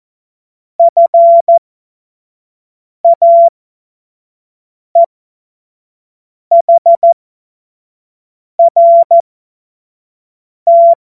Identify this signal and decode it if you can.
Morse code, heard as a keyed tone, about 14 wpm, spaced as for 5 wpm, 680 Hz. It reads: FAEHRT
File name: Morsekurs / Witze / Witz 01. März 2023 25 F70